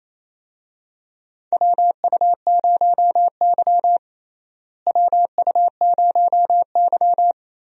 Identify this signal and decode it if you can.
Morse code, heard as a keyed tone, about 27 wpm, 700 Hz. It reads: WU0Y WU0Y
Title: USA Call Signs - 4 Characters - Repeated 2x - 28wpm